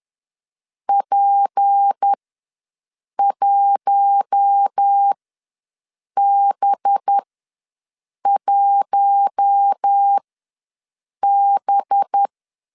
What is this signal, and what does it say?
Morse code, heard as a keyed tone, about 10 wpm, spaced as for 6 wpm, 800 Hz. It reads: P1B1B
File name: LEZIONI / 6 PKX0 / 6 P ws 10wpm 3751